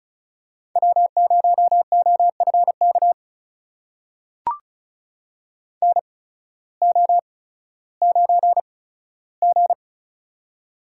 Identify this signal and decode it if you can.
Morse code, heard as a keyed tone, about 34 wpm, spaced as for 16 wpm, 700 Hz. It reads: W0OFK NO9G
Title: USA Call Signs - 8x Character Spacing - 35wpm